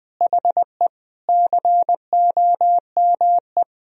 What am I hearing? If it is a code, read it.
Morse code, heard as a keyed tone, about 20 wpm, 700 Hz. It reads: HE COME
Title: Sentences from Top 300 Words and Abbreviations 20wpm